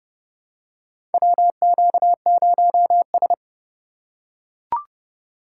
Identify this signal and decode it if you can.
Morse code, heard as a keyed tone, about 30 wpm, 700 Hz. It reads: WQ0S
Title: USA Call Signs - 4 Characters - 4x Character Spacing - 30wpm